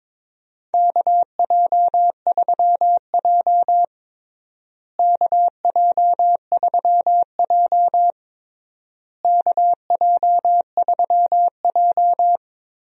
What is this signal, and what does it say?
Morse code, heard as a keyed tone, about 22 wpm, 700 Hz. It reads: KJ3J KJ3J KJ3J
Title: USA Call Signs - 4 Characters - Repeated 3x - 22wpm